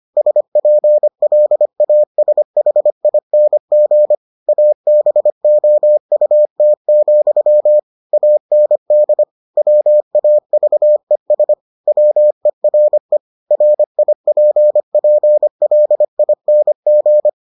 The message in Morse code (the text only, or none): SPLASHING ABOUT, AND WAVES WERE RIPPLING